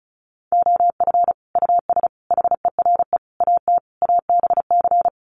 Morse code, heard as a keyed tone, about 34 wpm, 700 Hz. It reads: OF US HERE AT ABC
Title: Sets of 5 Words 35wpm